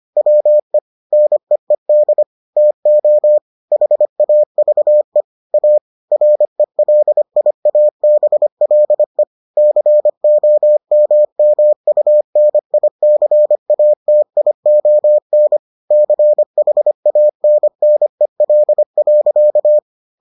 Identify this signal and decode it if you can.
Morse code, heard as a keyed tone, about 25 wpm, 600 Hz. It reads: WE NEED TO HAVE A RELIABLE COMMUNICATION CHANNEL.